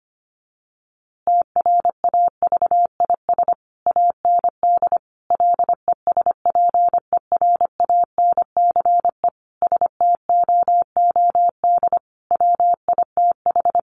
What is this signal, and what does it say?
Morse code, heard as a keyed tone, about 25 wpm, 700 Hz. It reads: TRAVIS AND LESPERANCE STOOD WITH
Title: A Sound Of Thunder 25wpm